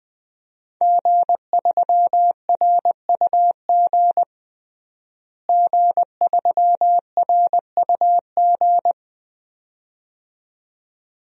Morse code, heard as a keyed tone, about 20 wpm, 700 Hz. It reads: G3RUG G3RUG
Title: DXCC-100 Call Signs - 5 Characters - Repeated 2x - 20wpm